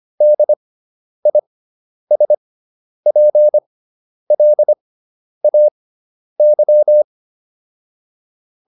Morse code, heard as a keyed tone, about 25 wpm, spaced as for 10 wpm, 600 Hz. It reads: DISPLAY